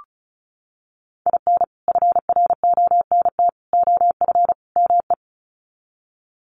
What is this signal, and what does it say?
Morse code, heard as a keyed tone, about 34 wpm, 700 Hz. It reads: IN FRONT OF ME
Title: Sets of 4 Words 35wpm